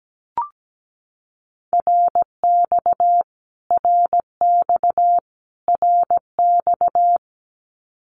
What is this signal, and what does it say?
Morse code, heard as a keyed tone, about 17 wpm, 700 Hz. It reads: RX RX RX
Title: Abbreviations Repeated 3x - 17wpm